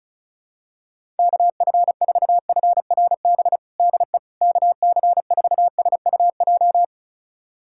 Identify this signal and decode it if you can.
Morse code, heard as a keyed tone, about 36 wpm, 700 Hz. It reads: KF4FRB DE KC4SUJ
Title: USA Callsign DE Callsign - 6 Characters - 35wpm